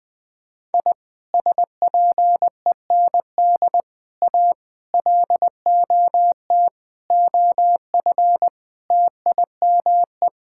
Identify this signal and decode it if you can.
Morse code, heard as a keyed tone, about 20 wpm, 700 Hz. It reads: I SPEND A LOT OF TIME
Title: Sentences with 5-Word Phrases - 20wpm